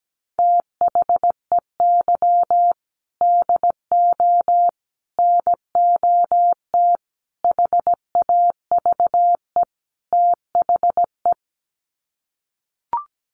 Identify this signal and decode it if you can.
Morse code, heard as a keyed tone, about 17 wpm, 700 Hz. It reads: THEY DO NOT HAVE THE
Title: Sets of 5 Words 17wpm